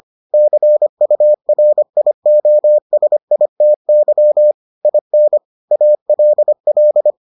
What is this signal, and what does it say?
Morse code, heard as a keyed tone, about 25 wpm, 600 Hz. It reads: CURIOSITY IN ALL